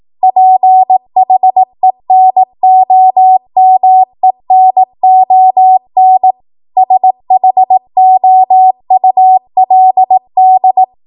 Morse code, heard as a keyed tone, about 18 wpm, 750 Hz. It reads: PHENOMENON SHOULD
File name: Morse / 18WPM